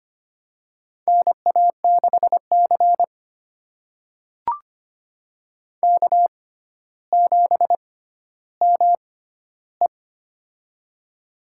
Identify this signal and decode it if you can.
Morse code, heard as a keyed tone, about 25 wpm, spaced as for 13 wpm, 700 Hz. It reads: NA6C K7ME
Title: USA Call Signs - 4 Characters - 6x Character Spacing - 25wpm